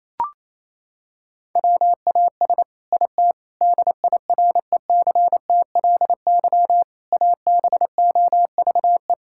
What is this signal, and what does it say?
Morse code, heard as a keyed tone, about 28 wpm, 700 Hz. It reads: WAS IT DIRECTLY ABOVE
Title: Sentences from Top 1000 Words - Review Part 5 - 28wpm